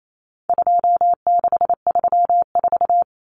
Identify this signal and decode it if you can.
Morse code, heard as a keyed tone, about 28 wpm, 700 Hz. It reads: 2634
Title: Numbers - 4-Digit 28wpm